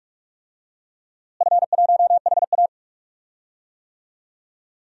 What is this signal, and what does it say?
Morse code, heard as a keyed tone, about 46 wpm, 700 Hz. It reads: F1HA
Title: DXCC-100 Call Signs - 4 Characters - Repeated 2x - 45wpm